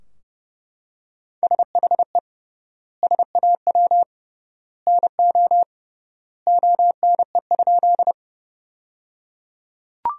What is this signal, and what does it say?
Morse code, heard as a keyed tone, about 28 wpm, 700 Hz. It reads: SHE SAW NO ONE?
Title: 3x Word-Spacing - Sentences from Top 300 Words - 30wpm